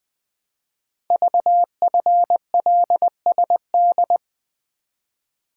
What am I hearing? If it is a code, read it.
Morse code, heard as a keyed tone, about 20 wpm, 700 Hz. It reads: VFLSD
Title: Code Groups - 5 Letters - 20wpm